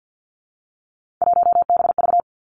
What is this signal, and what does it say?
Morse code, heard as a keyed tone, about 50 wpm, 700 Hz. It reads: 1BV
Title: Three Challenging Letter-Number Combinations 50wpm